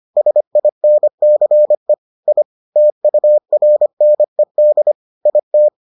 Morse code, heard as a keyed tone, about 25 wpm, 600 Hz. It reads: SINCE I TURNED IT